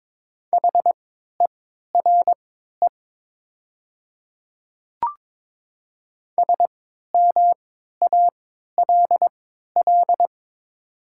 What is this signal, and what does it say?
Morse code, heard as a keyed tone, about 22 wpm, spaced as for 12 wpm, 700 Hz. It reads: HERE SMALL
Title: 3x Character Spacing - Top 200 Words - 22wpm